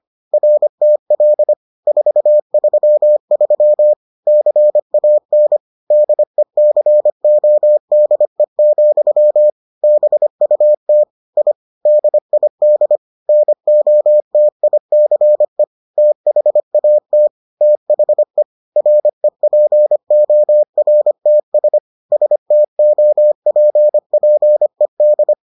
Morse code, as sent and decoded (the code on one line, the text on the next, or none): .-. - .-.. / ....- ...-- ...-- / -.-. .- -. / -.. -.-. --- -.. --..-- / -... ..- - / .. / -.. .. -.. / -. --- - .. -.-. / - .... .- - / - .... / .-. .--. --- .-. - ... / ... - --- .--. .--. -..
RTL 433 CAN DECODE, BUT I DID NOTICE THAT THE REPORTS STOPPED